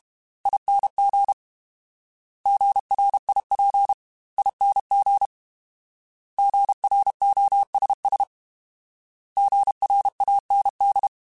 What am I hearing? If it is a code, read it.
Morse code, heard as a keyed tone, about 32 wpm, 800 Hz. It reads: ING GRIP ING GROSS GRAND
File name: Alphabetical / Alphabetical mp3s / Gs @30